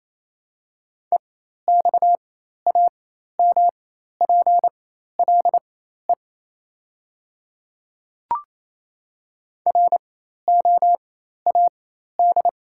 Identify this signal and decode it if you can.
Morse code, heard as a keyed tone, about 28 wpm, spaced as for 13 wpm, 700 Hz. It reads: EXAMPLE ROAD
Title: 4x Character Spacing - Top 500 Words - 28wpm